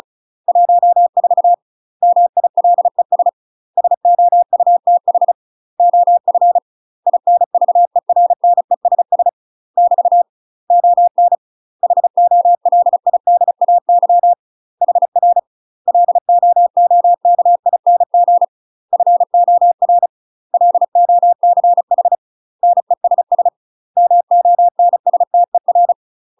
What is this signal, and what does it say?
Morse code, heard as a keyed tone, about 36 wpm, 700 Hz. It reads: 14 MILES SOUTH OF INVERNESS <BT> ON HOLIDAY HR LOOKING FOR LOCH NESS MONSTER
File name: UK MASTER QSOs TRAINING / 35WPM / UK MASTER QSOs 0007